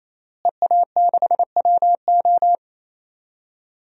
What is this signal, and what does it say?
Morse code, heard as a keyed tone, about 28 wpm, 700 Hz. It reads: EA6WO
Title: DXCC-100 Call Signs - 5 Characters - Repeated 3x - 28wpm